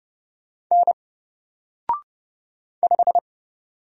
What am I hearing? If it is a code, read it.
Morse code, heard as a keyed tone, about 30 wpm, 700 Hz. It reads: N 5
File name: Single-Letter-Number-30wpm